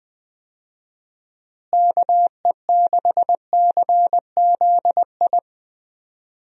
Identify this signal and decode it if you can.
Morse code, heard as a keyed tone, about 21 wpm, 700 Hz. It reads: KE6CZI
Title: USA Call Signs - 6 Characters - Repeated 2x - 20wpm